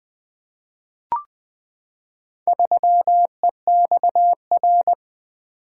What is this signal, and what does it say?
Morse code, heard as a keyed tone, about 20 wpm, 700 Hz. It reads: 3EXR
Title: Four Letter-Number Combinations - 20wpm